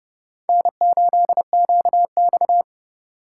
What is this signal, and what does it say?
Morse code, heard as a keyed tone, about 29 wpm, 700 Hz. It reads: N8QX
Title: USA Call Signs - 4 Characters - 4x Character Spacing - 30wpm